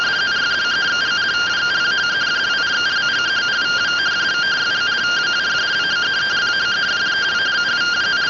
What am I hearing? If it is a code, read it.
Two-tone teleprinter signal, about 170 Hz shift.